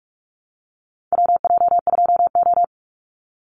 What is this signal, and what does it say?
Morse code, heard as a keyed tone, about 44 wpm, 700 Hz. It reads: WJ2O
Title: Call Signs - 45wpm